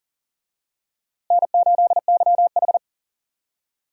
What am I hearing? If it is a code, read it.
Morse code, heard as a keyed tone, about 40 wpm, 700 Hz. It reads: N8YH
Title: USA Call Signs - 4 Characters - Repeated 2x - 40wpm